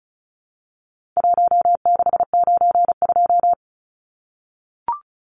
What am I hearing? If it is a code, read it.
Morse code, heard as a keyed tone, about 34 wpm, 700 Hz. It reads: 1692
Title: Numbers - 4-Digit 35wpm